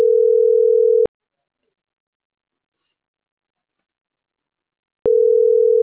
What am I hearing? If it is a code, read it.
Ringback tone (North America), 440 + 480 Hz.